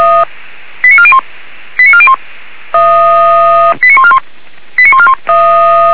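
Five-tone selective call (ZVEI), heard as quick runs of five tones, tones of about 70 ms. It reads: 89401 89401 89141 89141
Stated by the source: ZVEI five-tone selective call received by SDR, from Feuerwehr Orlamünde.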